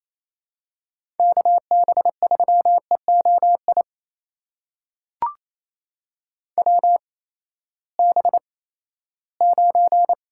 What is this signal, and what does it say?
Morse code, heard as a keyed tone, about 28 wpm, spaced as for 16 wpm, 700 Hz. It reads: KB3EOI WB9
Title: USA Call Signs - 6 Characters - 8x Character Spacing - 28wpm